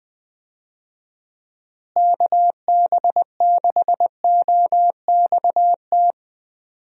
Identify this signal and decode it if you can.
Morse code, heard as a keyed tone, about 21 wpm, 700 Hz. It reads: KB6OXT
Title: USA Call Signs - 6 Characters - Repeated 2x - 20wpm